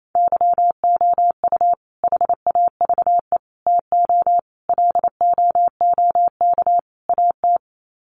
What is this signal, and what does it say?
Morse code, heard as a keyed tone, about 28 wpm, 700 Hz. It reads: YOU HAVE TO LOOK AT
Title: Sets of 5 Words Encore 28wpm